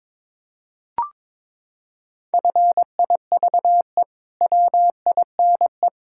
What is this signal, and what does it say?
Morse code, heard as a keyed tone, about 22 wpm, 700 Hz. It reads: FIVE WINE
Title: Pangram Challenge v2 - 22wpm